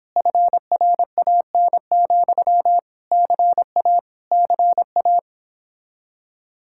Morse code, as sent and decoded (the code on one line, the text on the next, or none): ..-. .-. .- -. --..-- / -.-. .- / -.-. .-
FRAN, CA CA